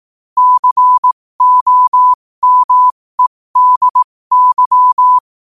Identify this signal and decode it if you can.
Morse code, heard as a keyed tone, about 18 wpm, spaced as for 15 wpm, 1000 Hz. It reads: COMEDY
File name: qotdfiles / 15wpm